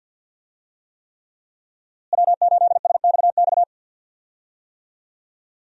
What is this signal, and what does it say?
Morse code, heard as a keyed tone, about 47 wpm, 700 Hz. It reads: W8SXX
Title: USA Call Signs - 5 Characters - Repeated 3x - 50wpm